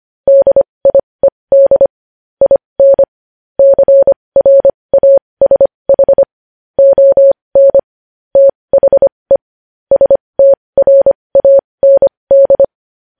Morse code, heard as a keyed tone, about 25 wpm, spaced as for 20 wpm, 550 Hz. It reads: DIED IN CRASH ON THE STRAND
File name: podcast / News headlines-1x-25x20